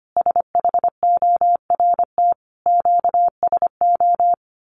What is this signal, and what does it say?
Morse code, heard as a keyed tone, about 25 wpm, 700 Hz. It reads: SHORT QSO
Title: Common QSO Phrases 25wpm